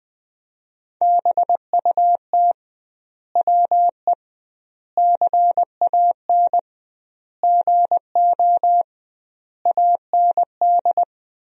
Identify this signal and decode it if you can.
Morse code, heard as a keyed tone, about 21 wpm, 700 Hz. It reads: BUT WE CAN GO AND
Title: 2x Word-Spacing - Sentences from Top 100 Words - 20wpm